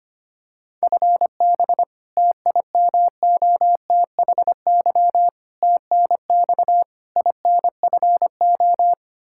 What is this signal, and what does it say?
Morse code, heard as a keyed tone, about 25 wpm, 700 Hz. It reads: FB TIMOTHY TNX INFO
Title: Practice QSO - Exchange 3 - More Info - 25wpm